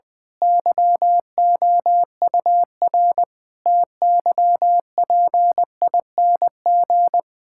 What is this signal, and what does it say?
Morse code, heard as a keyed tone, about 20 wpm, 700 Hz. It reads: YOUR TYPING
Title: Pangram Challenge v2 - 20wpm